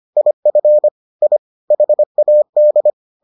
Morse code, heard as a keyed tone, about 25 wpm, 600 Hz. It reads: IF I HAD